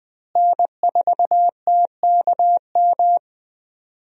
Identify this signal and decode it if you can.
Morse code, heard as a keyed tone, about 20 wpm, 700 Hz. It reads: N4TKM
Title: USA Call Signs - 5 Characters - Repeated 3x - 20wpm